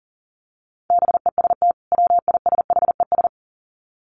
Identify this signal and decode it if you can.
Morse code, heard as a keyed tone, about 40 wpm, 700 Hz. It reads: BEST WISHES
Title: Common QSO Phrases 40wpm